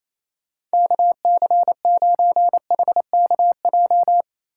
Morse code, heard as a keyed tone, about 34 wpm, 700 Hz. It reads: KC9HKJ
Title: Speed Racing - USA Call Signs 28wpm